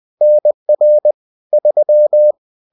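Morse code, heard as a keyed tone, about 20 wpm, 600 Hz. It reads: NR 3
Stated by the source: Bulletin sent in CW.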